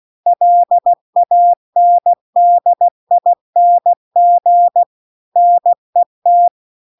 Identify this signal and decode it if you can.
Morse code, heard as a keyed tone, about 16 wpm, 700 Hz. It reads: LANDING NET